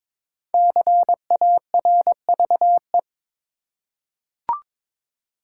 Morse code, heard as a keyed tone, about 22 wpm, 700 Hz. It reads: CARVE
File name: 5 Letter Words - First 3 Letters Given - 22wpm